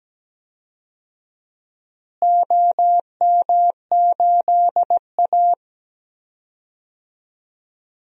Morse code, heard as a keyed tone, about 17 wpm, 700 Hz. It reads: OM8A
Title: DXCC-100 Call Signs - 4 Characters - Repeated 2x - 17wpm